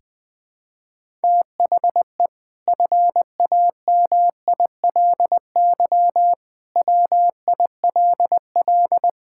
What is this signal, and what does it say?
Morse code, heard as a keyed tone, about 20 wpm, 700 Hz. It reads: THE FAMILY WILL
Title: Sentences from Top 1000 Words - Review Part 6 - 20wpm